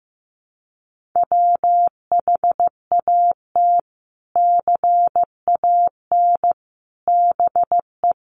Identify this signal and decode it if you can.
Morse code, heard as a keyed tone, about 15 wpm, 700 Hz. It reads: WHAT CAN BE
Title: Sentences from Top 500 Words 15wpm